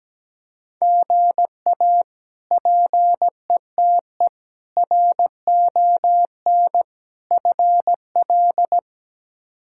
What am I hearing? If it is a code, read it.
Morse code, heard as a keyed tone, about 17 wpm, 700 Hz. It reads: GA PETE RON FL
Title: K1USN SST QSO Practice - 17wpm